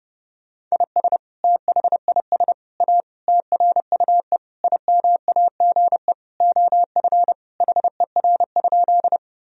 Morse code, heard as a keyed tone, about 30 wpm, 700 Hz. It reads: IS THIS A TRUE IMAGE OF HER?
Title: Sentences from Top 1000 Words - Review Part 4 - 30wpm